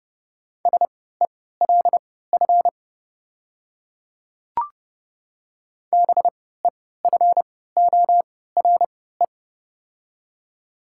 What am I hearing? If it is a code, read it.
Morse code, heard as a keyed tone, about 32 wpm, spaced as for 17 wpm, 700 Hz. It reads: SELF BEFORE
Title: 3x Character Spacing - Top 300 Words - 30wpm